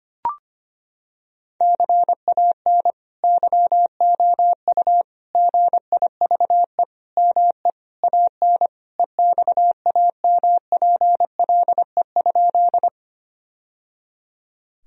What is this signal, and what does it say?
Morse code, heard as a keyed tone, about 25 wpm, 700 Hz. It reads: CAN YOU GIVE ME AN EXAMPLE?
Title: Sentences from Top 1000 Words - Review Part 1 - 25wpm